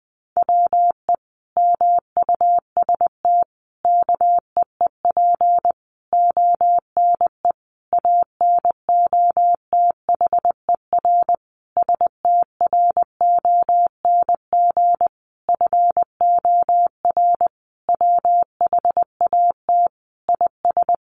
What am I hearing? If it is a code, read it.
Morse code, heard as a keyed tone, about 20 wpm, 700 Hz. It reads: WE MUST KEEP ONE ANOTHER STRONG FOR WHAT IS
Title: Sentences from Top 600 Words 20wpm